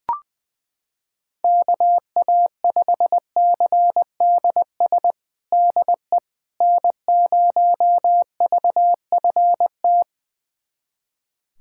Morse code, heard as a keyed tone, about 20 wpm, 700 Hz. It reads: KA5CDS DE N0VFT
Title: USA Callsign DE Callsign 20wpm